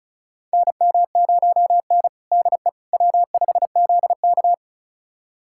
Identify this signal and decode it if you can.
Morse code, heard as a keyed tone, about 36 wpm, 700 Hz. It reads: NM0N DE W5ZK
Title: USA Callsign DE Callsign - 4 Characters - 35wpm